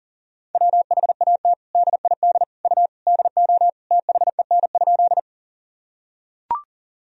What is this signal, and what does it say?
Morse code, heard as a keyed tone, about 40 wpm, 700 Hz. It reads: WHAT DID U DO THEN?
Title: Sentences from Top 100 Words and Abbreviations 40wpm